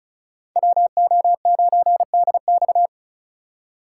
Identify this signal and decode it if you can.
Morse code, heard as a keyed tone, about 36 wpm, 700 Hz. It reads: WO9DX
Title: USA Call Signs - 5 Characters - 4x Character Spacing - 35wpm